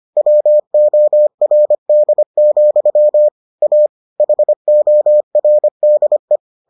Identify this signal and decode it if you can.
Morse code, heard as a keyed tone, about 25 wpm, 600 Hz. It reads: WORD, A HORDE